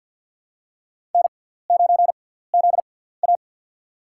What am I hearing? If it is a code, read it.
Morse code, heard as a keyed tone, about 52 wpm, spaced as for 17 wpm, 700 Hz. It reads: N9ZA
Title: USA Call Signs - 4 Characters - 6x Character Spacing - 50wpm